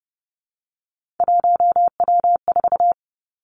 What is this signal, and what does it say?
Morse code, heard as a keyed tone, about 30 wpm, 700 Hz. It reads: 1W4
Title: Three Challenging Letter-Number Combinations 30wpm